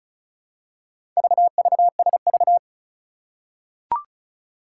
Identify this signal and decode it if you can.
Morse code, heard as a keyed tone, about 36 wpm, 700 Hz. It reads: VVSV
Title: Four Challenging Letter-Number Combinations 35wpm